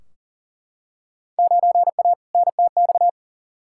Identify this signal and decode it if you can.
Morse code, heard as a keyed tone, about 40 wpm, 700 Hz. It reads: 9A NTX
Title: Field Day Exchanges Encore - 40wpm